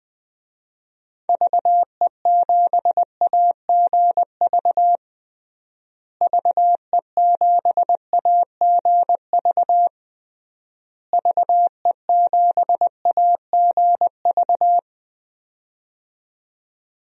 Morse code, heard as a keyed tone, about 20 wpm, 700 Hz. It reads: VE7AGV VE7AGV VE7AGV
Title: DXCC-100 Call Signs - 6 Characters - Repeated 3x - 20wpm